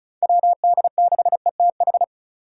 Morse code, heard as a keyed tone, about 35 wpm, 690 Hz. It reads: WD6ETH
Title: USA Call Signs - 6 Characters - 3x Character Spacing - 35wpm